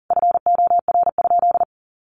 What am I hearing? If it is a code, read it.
Morse code, heard as a keyed tone, about 40 wpm, 700 Hz. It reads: FOR?
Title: Sentences from Top 1400 Words 40wpm